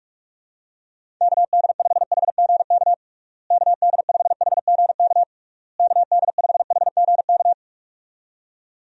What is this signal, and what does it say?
Morse code, heard as a keyed tone, about 46 wpm, 700 Hz. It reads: KD5HGK KD5HGK KD5HGK
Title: USA Call Signs - 6 Characters - Repeated 3x - 45wpm